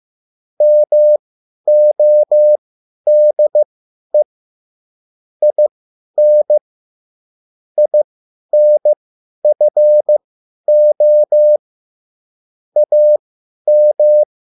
Morse code, heard as a keyed tone, about 15 wpm, spaced as for 10 wpm, 600 Hz. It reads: MODE IN INFO AM
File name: cwbook